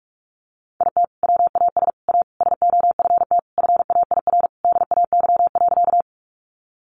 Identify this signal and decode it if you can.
Morse code, heard as a keyed tone, about 44 wpm, 700 Hz. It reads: IT WAS A SOFT FAIR DAY.
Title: Sentences from Top 1500 Words 45wpm